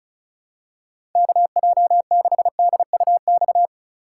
Morse code, heard as a keyed tone, about 36 wpm, 700 Hz. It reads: KJ6DUX
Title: USA Call Signs - 6 Characters - 2x Character Spacing - 35wpm